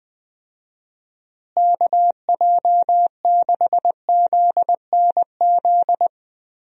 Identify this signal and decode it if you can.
Morse code, heard as a keyed tone, about 19 wpm, 700 Hz. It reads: KJ6ZNZ
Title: USA Call Signs - 6 Characters - Repeated 3x - 20wpm